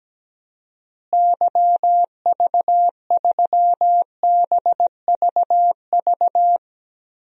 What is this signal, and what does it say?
Morse code, heard as a keyed tone, about 17 wpm, 700 Hz. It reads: YV3BVV
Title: DXCC-100 Call Signs - 6 Characters - Repeated 3x - 17wpm